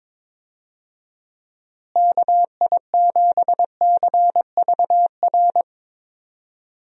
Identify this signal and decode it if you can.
Morse code, heard as a keyed tone, about 22 wpm, 700 Hz. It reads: KI7CVR
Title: USA Call Signs - 6 Characters - Repeated 2x - 22wpm